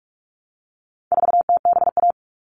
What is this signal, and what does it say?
Morse code, heard as a keyed tone, about 44 wpm, 700 Hz. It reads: 4TBA
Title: All Cut Numbers - 4-Digit 45wpm